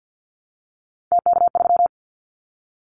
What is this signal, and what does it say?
Morse code, heard as a keyed tone, about 50 wpm, 700 Hz. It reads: TK3
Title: Callsign Prefixes for Top 100 DXCC - 50wpm